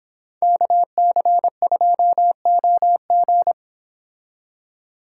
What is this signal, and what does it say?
Morse code, heard as a keyed tone, about 26 wpm, 700 Hz. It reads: KC2OG